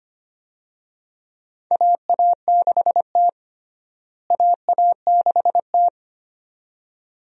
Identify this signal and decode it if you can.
Morse code, heard as a keyed tone, about 25 wpm, 700 Hz. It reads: AA6T AA6T
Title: USA Call Signs - 4 Characters - Repeated 2x - 25wpm